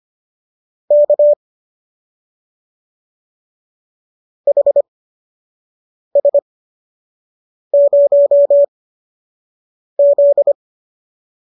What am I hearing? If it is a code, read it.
Morse code, heard as a keyed tone, about 25 wpm, spaced as for 6 wpm, 570 Hz. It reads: K HS0Z